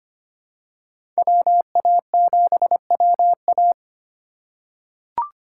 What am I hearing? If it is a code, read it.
Morse code, heard as a keyed tone, about 25 wpm, 700 Hz. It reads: WA7WA
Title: USA Call Signs - 5 Characters - 4x Character Spacing - 25wpm